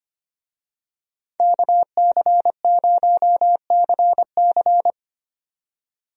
Morse code, heard as a keyed tone, about 25 wpm, 700 Hz. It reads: KC0CC
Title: USA Call Signs - 5 Characters - 25wpm